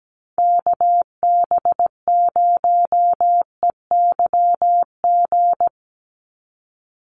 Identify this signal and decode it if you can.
Morse code, heard as a keyed tone, about 17 wpm, 700 Hz. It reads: KB0EYG
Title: USA Call Signs 17wpm